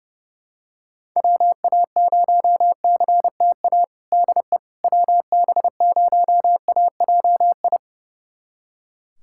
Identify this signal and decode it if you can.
Morse code, heard as a keyed tone, about 32 wpm, 700 Hz. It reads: WA0CTA DE WB0AJI
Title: USA Callsign DE Callsign - 6 Characters - 30wpm